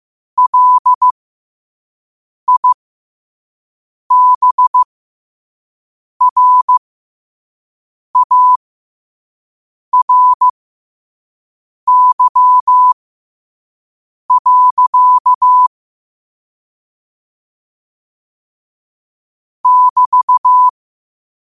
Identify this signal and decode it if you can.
Morse code, heard as a keyed tone, about 15 wpm, spaced as for 5 wpm, 1000 Hz. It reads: LIBRARY. <BT>